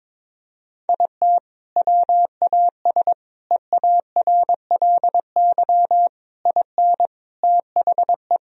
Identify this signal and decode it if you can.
Morse code, heard as a keyed tone, about 22 wpm, 700 Hz. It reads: IT WAS EARLY IN THE
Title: Sentences from Top 1000 Words - Review Part 2 - 22wpm